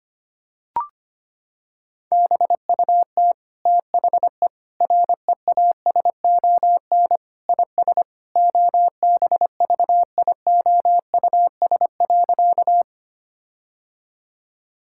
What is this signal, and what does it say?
Morse code, heard as a keyed tone, about 25 wpm, 700 Hz. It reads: BUT THE REASON IS OBVIOUS.
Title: Sentences from Top 1600 Words 25wpm